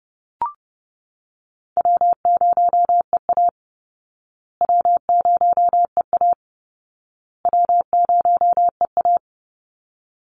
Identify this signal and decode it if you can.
Morse code, heard as a keyed tone, about 30 wpm, 700 Hz. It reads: W0EA W0EA W0EA
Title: Easy Call Signs Repeated 3x - 30wpm